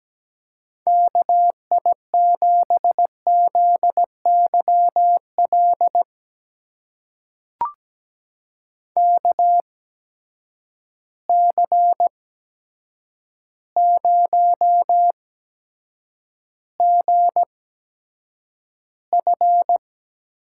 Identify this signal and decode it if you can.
Morse code, heard as a keyed tone, about 17 wpm, spaced as for 8 wpm, 700 Hz. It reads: KI7ZYL KC0GF
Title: USA Call Signs - 6 Characters - 8x Character Spacing - 17wpm